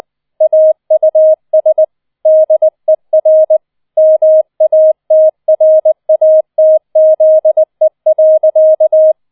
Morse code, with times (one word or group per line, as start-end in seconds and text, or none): 0.4-1.8 AUS
2.3-3.6 DER
4.0-9.1 MATRATZE.